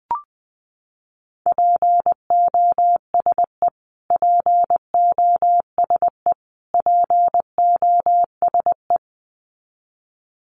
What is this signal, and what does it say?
Morse code, heard as a keyed tone, about 20 wpm, 700 Hz. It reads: POSE POSE POSE
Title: Top 500 Words Repeated 20wpm